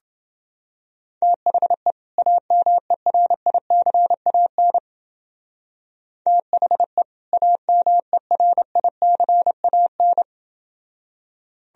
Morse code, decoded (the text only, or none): THE AMERICAN THE AMERICAN